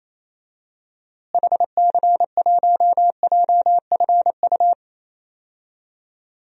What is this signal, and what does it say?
Morse code, heard as a keyed tone, about 28 wpm, 700 Hz. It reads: HC1JFU
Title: DXCC-100 Call Signs - 6 Characters - Repeated 3x - 28wpm